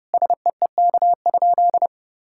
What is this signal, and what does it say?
Morse code, heard as a keyed tone, about 30 wpm, 700 Hz. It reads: SEEK?
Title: Sentences from Top 1600 Words 30wpm